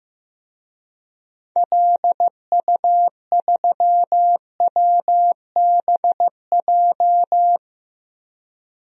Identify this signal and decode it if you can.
Morse code, heard as a keyed tone, about 15 wpm, 700 Hz. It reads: LU3WBJ
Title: DXCC-100 Call Signs - 6 Characters - Repeated 3x - 15wpm